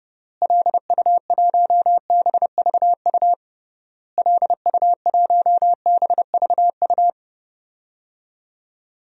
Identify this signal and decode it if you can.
Morse code, heard as a keyed tone, about 30 wpm, 700 Hz. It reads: LU1BVU LU1BVU
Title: DXCC-100 Call Signs - 6 Characters - Repeated 3x - 30wpm